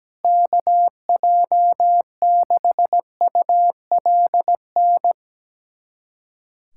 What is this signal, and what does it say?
Morse code, heard as a keyed tone, about 17 wpm, 700 Hz. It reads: KJ6ULN
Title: USA Callsign DE Callsign - 6 Characters - 17wpm